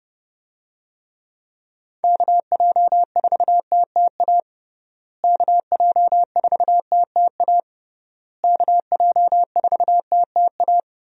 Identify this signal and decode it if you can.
Morse code, heard as a keyed tone, about 32 wpm, 700 Hz. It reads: KJ4TTA KJ4TTA KJ4TTA
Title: USA Call Signs - 6 Characters - Repeated 3x - 30wpm